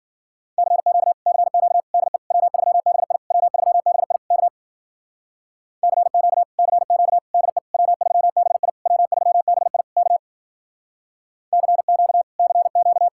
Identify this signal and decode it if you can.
Morse code, heard as a keyed tone, about 50 wpm, 700 Hz. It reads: CQ CQ DE W3BI W3BI K CQ CQ DE W3BI W3BI K CQ CQ
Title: Speed-Racing Common QSO Phrases - 40wpm